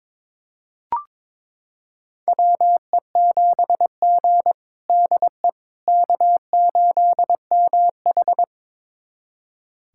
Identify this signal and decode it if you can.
Morse code, heard as a keyed tone, about 22 wpm, 700 Hz. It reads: WE7G DE K8MH
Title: USA Callsign DE Callsign - 4 Characters - 22wpm